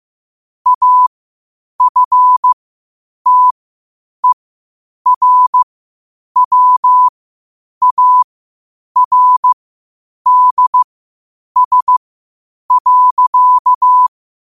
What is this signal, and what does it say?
Morse code, heard as a keyed tone, about 15 wpm, spaced as for 8 wpm, 1000 Hz. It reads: AFTERWARDS.
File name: qotdfiles / 8wpm